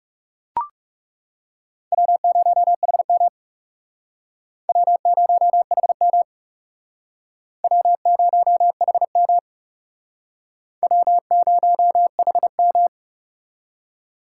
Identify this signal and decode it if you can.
Morse code, heard as a keyed tone, about 38 wpm, 700 Hz. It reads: W0HM W0HM W0HM W0HM